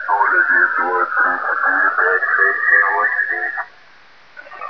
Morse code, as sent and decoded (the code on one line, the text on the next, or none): none